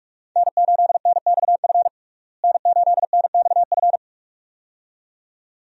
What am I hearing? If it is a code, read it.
Morse code, heard as a keyed tone, about 42 wpm, 700 Hz. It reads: N8NXF N8NXF
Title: USA Call Signs - 5 Characters - Repeated 2x - 45wpm